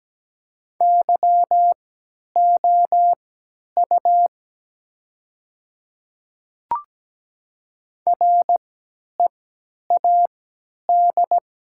Morse code, heard as a keyed tone, about 17 wpm, spaced as for 10 wpm, 700 Hz. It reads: YOU READ
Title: 3x Character Spacing - Top 200 Words - 17wpm